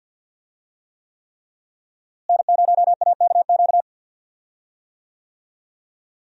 Morse code, heard as a keyed tone, about 53 wpm, 700 Hz. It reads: N0AKQ
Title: USA Call Signs - 5 Characters - Repeated 3x - 50wpm